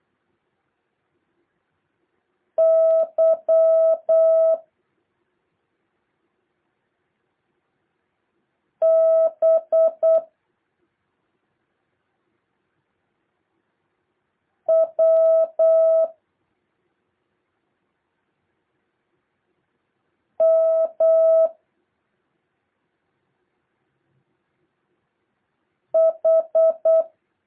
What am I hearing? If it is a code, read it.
Morse code, heard as a keyed tone, about 8 wpm, 650 Hz. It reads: Y B W M H
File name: jh8cbh / 2023cw